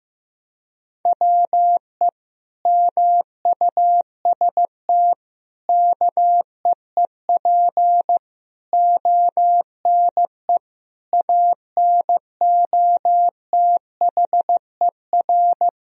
Morse code, heard as a keyed tone, about 15 wpm, 700 Hz. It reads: WE MUST KEEP ONE ANOTHER
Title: Sentences from Top 1000 Words - Review Part 3 - 15wpm